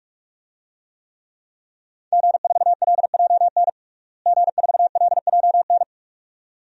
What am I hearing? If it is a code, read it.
Morse code, heard as a keyed tone, about 42 wpm, 700 Hz. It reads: G4LJN G4LJN
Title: DXCC-100 Call Signs - 5 Characters - Repeated 2x - 45wpm